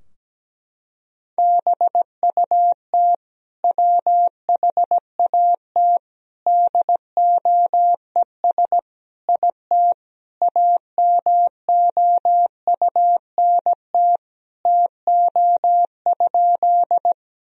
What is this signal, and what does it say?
Morse code, heard as a keyed tone, about 17 wpm, 700 Hz. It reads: BUT WHAT DOES IT AMOUNT TO?
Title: Sentences from Top 1000 Words - Review Part 4 - 17wpm